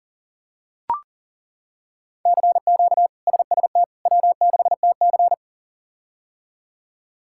Morse code, heard as a keyed tone, about 40 wpm, 700 Hz. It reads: CQ SST W6TC
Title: K1USN SST QSO Practice - 40wpm